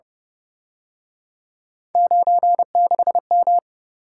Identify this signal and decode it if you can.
Morse code, heard as a keyed tone, about 30 wpm, 700 Hz. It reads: 96M
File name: Speed-Racing - Three Letter-Number Combinations - 25wpm